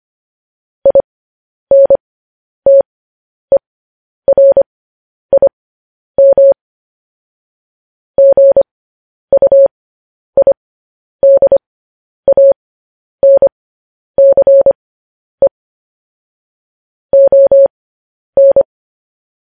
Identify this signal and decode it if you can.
Morse code, heard as a keyed tone, about 25 wpm, spaced as for 10 wpm, 550 Hz. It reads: INTERIM GUIDANCE ON
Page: news headlines in morse code at 10 wpm